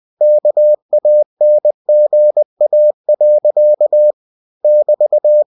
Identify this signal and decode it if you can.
Morse code, heard as a keyed tone, about 20 wpm, 600 Hz. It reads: KANGA. <BT>
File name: cw-books / winnie-the-pooh / winnie-20wpm-page-0066